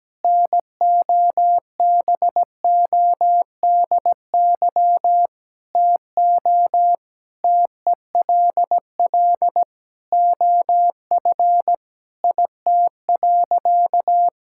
Morse code, heard as a keyed tone, about 17 wpm, 700 Hz. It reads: NOBODY TO TELL OF IT.
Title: Sentences from Top 1600 Words 17wpm